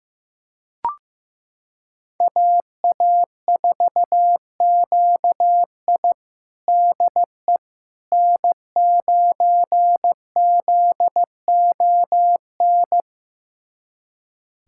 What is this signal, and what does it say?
Morse code, heard as a keyed tone, about 15 wpm, 700 Hz. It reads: AA4QI DE N9ZON
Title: USA Callsign DE Callsign - 5 Characters - 15wpm